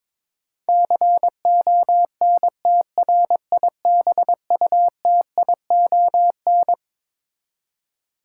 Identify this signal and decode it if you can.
Morse code, heard as a keyed tone, about 22 wpm, 700 Hz. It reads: CONTRIBUTION
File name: Twelve Letter Words - 22wpm